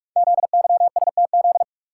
Morse code, heard as a keyed tone, about 44 wpm, 700 Hz. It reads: ZYSTZ
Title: Code Groups - 5 Letters - 45wpm